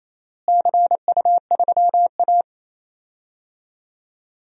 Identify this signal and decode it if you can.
Morse code, heard as a keyed tone, about 28 wpm, 700 Hz. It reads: CU3A
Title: DXCC-100 Call Signs - 4 Characters - Repeated 2x - 28wpm